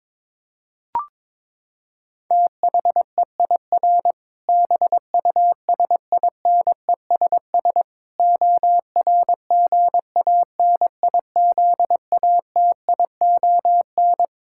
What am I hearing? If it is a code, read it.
Morse code, heard as a keyed tone, about 21 wpm, 700 Hz. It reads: THEIR BUSINESS ORGANIZATION
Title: Sentences from Top 1000 Words - Review Part 5 - 22wpm